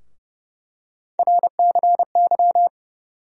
Morse code, heard as a keyed tone, about 30 wpm, 700 Hz. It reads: RCY
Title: USA Call Sign Postfixes - 3 Letters 30wpm